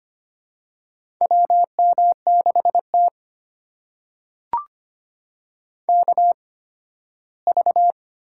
Morse code, heard as a keyed tone, about 25 wpm, spaced as for 15 wpm, 700 Hz. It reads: WM6T KV
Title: USA Call Signs - 4 Characters - 8x Character Spacing - 25wpm